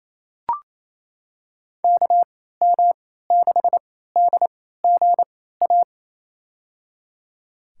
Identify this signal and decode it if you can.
Morse code, heard as a keyed tone, about 28 wpm, spaced as for 16 wpm, 700 Hz. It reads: KM6DGA
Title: USA Call Signs - 6 Characters - 3x Character Spacing - 28wpm